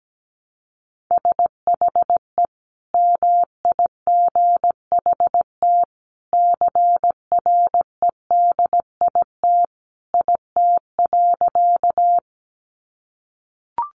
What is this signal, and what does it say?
Morse code, heard as a keyed tone, about 17 wpm, 700 Hz. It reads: SHE MIGHT CREDIT IT.
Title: Sentences from Top 1000 Words 17wpm